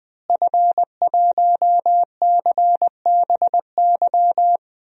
Morse code, heard as a keyed tone, about 21 wpm, 700 Hz. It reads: F1CBY
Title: DXCC-100 Call Signs - 5 Characters - Repeated 3x - 20wpm